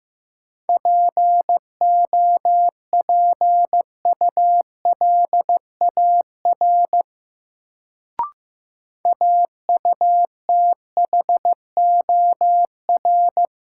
Morse code, heard as a keyed tone, about 15 wpm, 700 Hz. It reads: POPULAR AUTHOR
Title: Sentences from Top 1200 Words - New Word Drill Alt - 15wpm